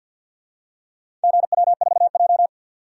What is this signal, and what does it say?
Morse code, heard as a keyed tone, about 48 wpm, 700 Hz. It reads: GW4J
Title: DXCC-100 Call Signs - 4 Characters - 50wpm